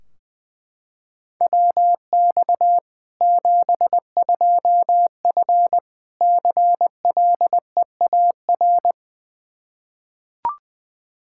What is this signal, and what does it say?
Morse code, heard as a keyed tone, about 21 wpm, 700 Hz. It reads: WX 72F CLEAR
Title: Practice QSO - Exchange 2 - Info - 20wpm